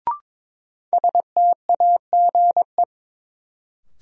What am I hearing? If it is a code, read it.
Morse code, heard as a keyed tone, about 22 wpm, 700 Hz. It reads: STAGE
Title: Sentences from Top 1200 Words - New Word Drill Alt - 22wpm